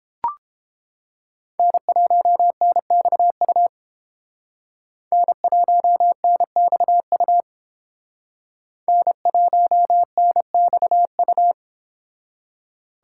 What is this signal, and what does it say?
Morse code, heard as a keyed tone, about 30 wpm, 700 Hz. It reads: N1NXU N1NXU N1NXU